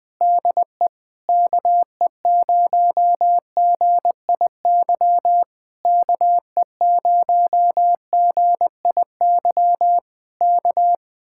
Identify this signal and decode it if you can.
Morse code, heard as a keyed tone, about 20 wpm, 700 Hz. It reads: DE KE0GIY KE0GIY K
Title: Speed-Racing Common QSO Phrases - 17wpm